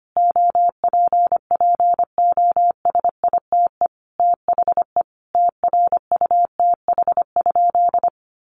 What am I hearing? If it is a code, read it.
Morse code, heard as a keyed tone, about 25 wpm, 700 Hz. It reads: OPPOSITE THE TRUTH?
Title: Sentences from Top 1300 Words 25wpm